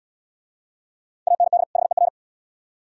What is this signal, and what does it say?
Morse code, heard as a keyed tone, about 65 wpm, 700 Hz. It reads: AND HER